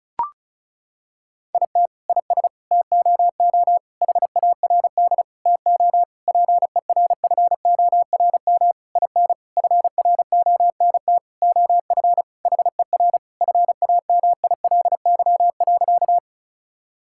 Morse code, heard as a keyed tone, about 36 wpm, 700 Hz. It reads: IT IS TOO HARD TO PERFORM IN FRONT OF HER FAMILY.
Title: Sentences from Top 1000 Words - Review Part 5 - 35wpm